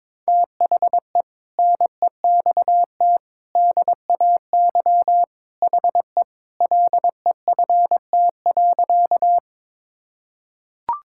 Morse code, heard as a keyed tone, about 22 wpm, 700 Hz. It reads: THE NEXT DAY HE LEFT.
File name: practice / 22wpm / Sentences with 3-Word Phrases - 22wpm